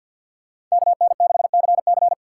Morse code, heard as a keyed tone, about 50 wpm, 700 Hz. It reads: KN6CC
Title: USA Call Signs - 5 Characters - 2x Character Spacing - 50wpm